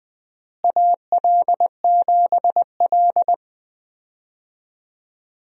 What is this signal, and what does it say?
Morse code, heard as a keyed tone, about 20 wpm, 700 Hz. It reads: AL7L
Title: USA Call Signs - 4 Characters - Repeated 3x - 20wpm